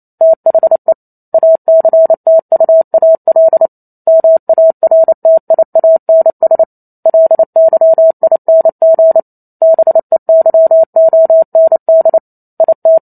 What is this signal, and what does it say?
Morse code, heard as a keyed tone, about 28 wpm, 650 Hz. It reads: THE ACTUAL MARTIANS LYING BEYOND IT
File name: Chapter 2 - 28 WPM